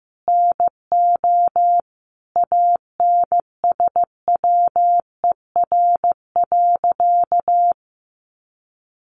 Morse code, heard as a keyed tone, about 15 wpm, 700 Hz. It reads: NO ANSWER.
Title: Sentences from Top 1000 Words 15wpm